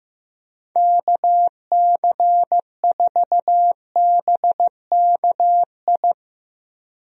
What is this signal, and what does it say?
Morse code, heard as a keyed tone, about 15 wpm, 700 Hz. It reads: KC4BKI
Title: USA Call Signs - 6 Characters - Repeated 3x - 15wpm